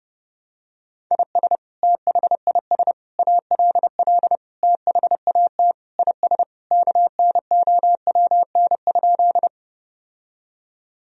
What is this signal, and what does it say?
Morse code, heard as a keyed tone, about 30 wpm, 700 Hz. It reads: IS THIS ALL THAT IS KNOWN?
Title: Sentences from Top 1000 Words - Review Part 3 - 30wpm